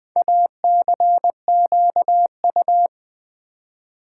Morse code, heard as a keyed tone, about 20 wpm, 700 Hz. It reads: ACQU
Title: Build A Word - 4 Letters At A Time - 11-Letter Words - 20wpm